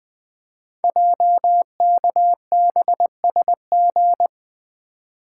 Morse code, heard as a keyed tone, about 20 wpm, 700 Hz. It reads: JKBSG
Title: Code Groups - 5 Letters - 20wpm